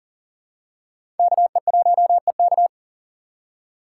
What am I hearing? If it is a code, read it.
Morse code, heard as a keyed tone, about 40 wpm, 700 Hz. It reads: KE1EK
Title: USA Call Signs - 5 Characters - 2x Character Spacing - 40wpm